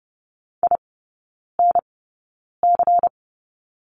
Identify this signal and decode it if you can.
Morse code, heard as a keyed tone, about 30 wpm, spaced as for 9 wpm, 690 Hz. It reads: INC
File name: Lesson 16e - Letter c - Three Characters 30wpm